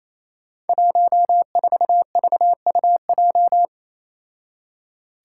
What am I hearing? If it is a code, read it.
Morse code, heard as a keyed tone, about 28 wpm, 700 Hz. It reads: 14VUJ
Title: Five Challenging Letter-Number Combinations 28wpm